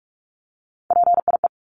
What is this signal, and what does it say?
Morse code, heard as a keyed tone, about 44 wpm, 700 Hz. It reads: PIE
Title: Top 1751-2000 Words 45wpm